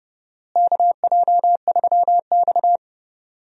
Morse code, heard as a keyed tone, about 30 wpm, 700 Hz. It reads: KJ3X
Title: USA Call Signs - 4 Characters - Repeated 2x - 30wpm